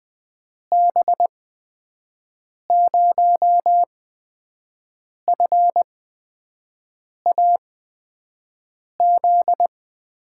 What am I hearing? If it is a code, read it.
Morse code, heard as a keyed tone, about 20 wpm, spaced as for 5 wpm, 700 Hz. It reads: B0FAZ
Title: USA Call Signs - 6 Characters - 8x Character Spacing - 20wpm